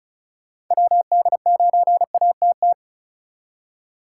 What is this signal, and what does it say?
Morse code, heard as a keyed tone, about 45 wpm, 700 Hz. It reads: WD9ATT